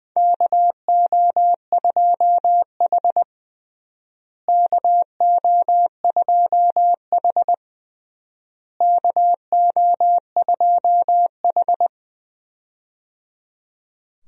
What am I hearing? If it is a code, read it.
Morse code, heard as a keyed tone, about 20 wpm, 700 Hz. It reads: KO2H KO2H KO2H
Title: USA Call Signs - 4 Characters - Repeated 3x - 20wpm